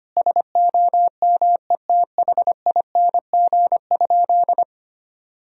Morse code, heard as a keyed tone, about 25 wpm, 700 Hz. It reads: SOMETHING?
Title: Sentences from Top 1000 Words - Review Part 2 - 25wpm